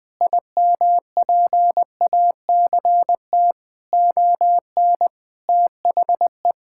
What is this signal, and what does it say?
Morse code, heard as a keyed tone, about 20 wpm, 700 Hz. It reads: IMPACT ON THE
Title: Sentences from Top 1000 Words - Review Part 6 - 20wpm